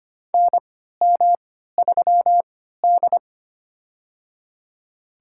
Morse code, heard as a keyed tone, about 25 wpm, spaced as for 14 wpm, 700 Hz. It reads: NM3D
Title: USA Call Signs - 4 Characters - 3x Character Spacing - 25wpm